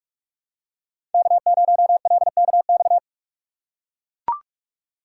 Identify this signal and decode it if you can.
Morse code, heard as a keyed tone, about 44 wpm, 700 Hz. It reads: K0LKX
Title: USA Call Signs - 5 Characters - 6x Character Spacing - 45wpm